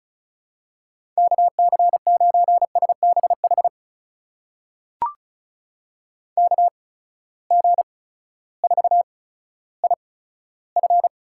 Morse code, heard as a keyed tone, about 36 wpm, spaced as for 16 wpm, 700 Hz. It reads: KC9SBH KG4IF
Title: USA Call Signs - 8x Character Spacing - 35wpm